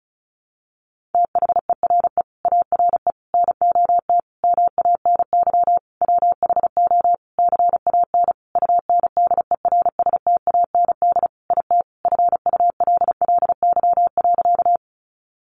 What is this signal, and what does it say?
Morse code, heard as a keyed tone, about 34 wpm, 700 Hz. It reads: THERE ARE NOT MANY WHO CAN UNDERSTAND IT FULLY.
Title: Sentences from Top 1300 Words 35wpm